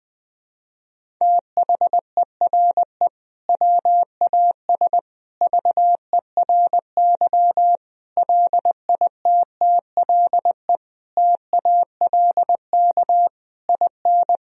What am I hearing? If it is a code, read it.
Morse code, heard as a keyed tone, about 20 wpm, 700 Hz. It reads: THERE WAS VERY LITTLE TALK IN
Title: Sentences from Top 1000 Words - Review Part 1 - 20wpm